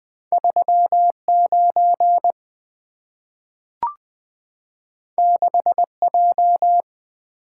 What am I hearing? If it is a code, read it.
Morse code, heard as a keyed tone, about 20 wpm, 700 Hz. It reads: 39 6J
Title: Two Letter-Number Combinations - 20wpm